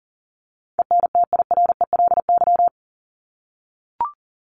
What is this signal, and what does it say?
Morse code, heard as a keyed tone, about 40 wpm, 700 Hz. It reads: ENTIRELY
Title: LY Suffix - Top 100 Words with Base Words 40wpm